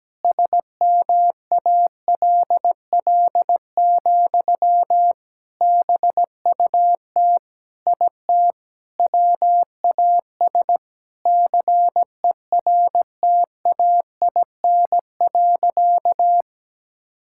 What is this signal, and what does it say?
Morse code, heard as a keyed tone, about 17 wpm, 700 Hz. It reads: SMALL, BUT IT WAS CERTAIN.
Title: Sentences from Top 1000 Words - Review Part 6 - 17wpm